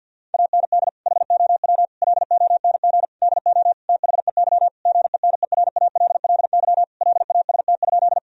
Morse code, heard as a keyed tone, about 50 wpm, 700 Hz. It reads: AND HOW LONG DO THEY GENERALLY LAST?
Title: Sentences from Top 1000 Words - Review Part 4 - 50wpm